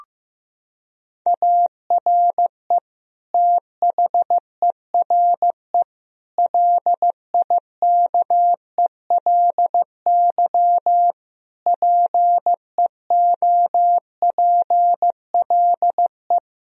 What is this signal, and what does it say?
Morse code, heard as a keyed tone, about 16 wpm, 700 Hz. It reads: ARE THERE LIKELY PEOPLE
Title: Sentences from Top 1000 Words - Review Part 4 - 15wpm